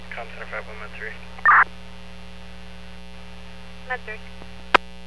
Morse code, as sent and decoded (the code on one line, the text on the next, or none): none